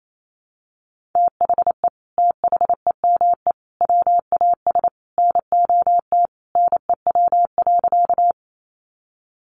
Morse code, heard as a keyed tone, about 28 wpm, 700 Hz. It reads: THE THEME WAS NOT NEW.
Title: Sentences from Top 1100 Words 28wpm